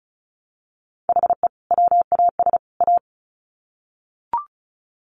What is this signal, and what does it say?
Morse code, heard as a keyed tone, about 34 wpm, 700 Hz. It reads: HE WAS A
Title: Sets of 3 Words 35wpm